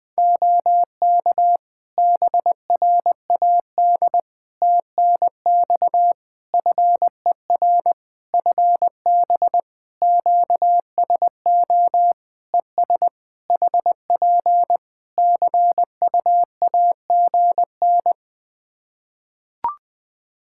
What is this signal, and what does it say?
Morse code, heard as a keyed tone, about 20 wpm, 700 Hz. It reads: OK BRAD TNX FER FB QSO ES HP CUAGN
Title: K1USN SST QSO Practice 20wpm